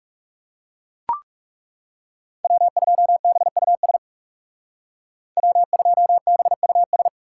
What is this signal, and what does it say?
Morse code, heard as a keyed tone, about 38 wpm, 700 Hz. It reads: W2BUS W2BUS